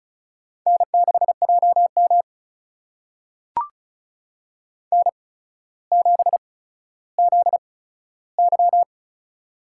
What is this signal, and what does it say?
Morse code, heard as a keyed tone, about 36 wpm, spaced as for 15 wpm, 700 Hz. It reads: N6JM N7ZY
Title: USA Call Signs - 4 Characters - 8x Character Spacing - 35wpm